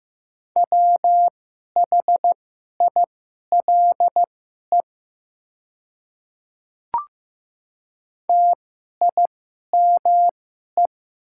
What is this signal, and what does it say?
Morse code, heard as a keyed tone, about 15 wpm, spaced as for 11 wpm, 700 Hz. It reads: WHILE TIME
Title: 2x Character Spacing - Top 300 Words - 15wpm